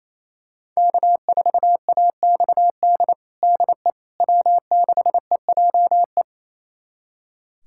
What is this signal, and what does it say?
Morse code, heard as a keyed tone, about 28 wpm, 700 Hz. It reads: K4AXD DE W6EJE
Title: USA Callsign DE Callsign 28wpm